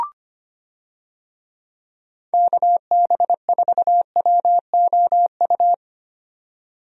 Morse code, heard as a keyed tone, about 25 wpm, 700 Hz. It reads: KB4WOU
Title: USA Call Signs - 6 Characters - 25wpm